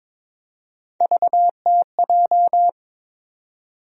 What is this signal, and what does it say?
Morse code, heard as a keyed tone, about 22 wpm, 700 Hz. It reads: VTJ
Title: USA Call Sign Postfixes - 3 Letters 22wpm